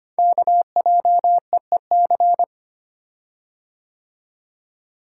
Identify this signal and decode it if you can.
Morse code, heard as a keyed tone, about 25 wpm, 690 Hz. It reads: KJEEC
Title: Code Groups - 5 Letters - 25wpm